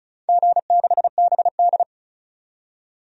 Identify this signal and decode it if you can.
Morse code, heard as a keyed tone, about 36 wpm, 700 Hz. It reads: G6BD
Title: DXCC-100 Call Signs - 4 Characters - 35wpm